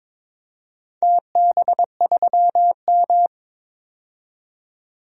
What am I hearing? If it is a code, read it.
Morse code, heard as a keyed tone, about 22 wpm, 700 Hz. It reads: TB3M
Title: DXCC-100 Call Signs - 4 Characters - 22wpm